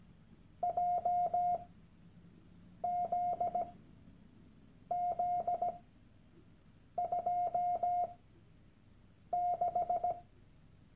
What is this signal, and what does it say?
Morse code, heard as a keyed tone, about 17 wpm, spaced as for 6 wpm, 710 Hz. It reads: JZZ26
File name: Workshop Dateien / CW-Online-Kurs / Lektion 001